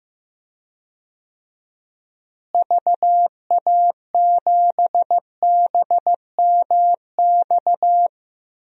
Morse code, heard as a keyed tone, about 15 wpm, 700 Hz. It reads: VA7BMX
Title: DXCC-100 Call Signs - 6 Characters - Repeated 3x - 15wpm